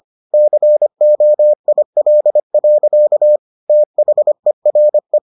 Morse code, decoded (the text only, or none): COIL. THERE